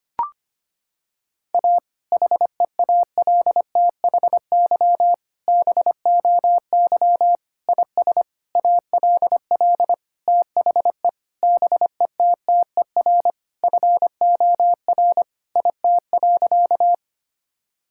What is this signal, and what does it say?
Morse code, heard as a keyed tone, about 25 wpm, 700 Hz. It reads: A HEALTHY BOY IS ALL THE BETTER FOR IT.
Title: Sentences from Top 1000 Words - Review Part 5 - 25wpm